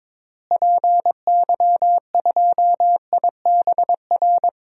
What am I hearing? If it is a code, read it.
Morse code, heard as a keyed tone, about 21 wpm, 700 Hz. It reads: PY2IBR
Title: DXCC-100 Call Signs - 6 Characters - Repeated 2x - 22wpm